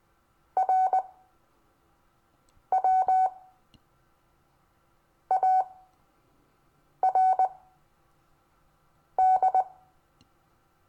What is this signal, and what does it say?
Morse code, heard as a keyed tone, about 19 wpm, spaced as for 5 wpm, 750 Hz. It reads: RWARD
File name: cw / Lesson26 SUMMARY